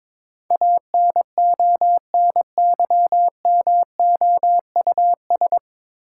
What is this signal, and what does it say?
Morse code, heard as a keyed tone, about 22 wpm, 700 Hz. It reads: ANONYMOUS
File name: 9 Letter Words - First 3 Letters Given - 22wpm